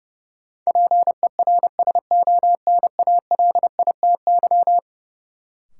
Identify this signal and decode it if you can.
Morse code, heard as a keyed tone, about 30 wpm, 700 Hz. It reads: PERSONALITY
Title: Sentences from Top 1400 Words - New Word Drill - 30wpm